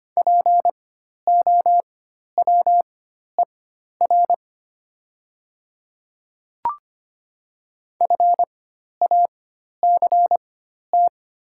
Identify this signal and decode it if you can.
Morse code, heard as a keyed tone, about 25 wpm, spaced as for 12 wpm, 700 Hz. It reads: POWER FACT
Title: 4x Character Spacing - Top 500 Words - 25wpm